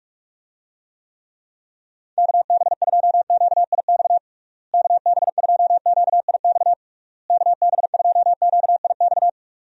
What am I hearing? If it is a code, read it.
Morse code, heard as a keyed tone, about 46 wpm, 700 Hz. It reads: KB2QIX KB2QIX KB2QIX
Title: USA Call Signs - 6 Characters - Repeated 3x - 45wpm